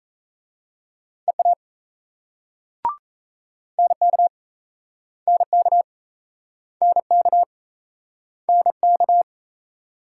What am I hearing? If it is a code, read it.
Morse code, heard as a keyed tone, about 39 wpm, 700 Hz. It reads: EA NK NK NK NK